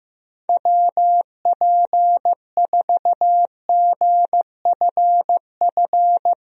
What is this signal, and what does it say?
Morse code, heard as a keyed tone, about 15 wpm, 700 Hz. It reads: WP4GFF
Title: USA Call Signs - 6 Characters - Repeated 3x - 15wpm